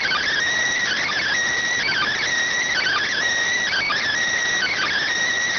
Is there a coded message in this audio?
Yes: Emergency Alert System (SAME) tones from 0.8 to 5.0 s, heard as harsh buzzing bursts.